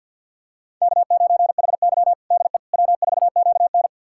Morse code, heard as a keyed tone, about 51 wpm, 700 Hz. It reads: K9HY DE W4QN
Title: USA Callsign DE Callsign - 4 Characters - 50wpm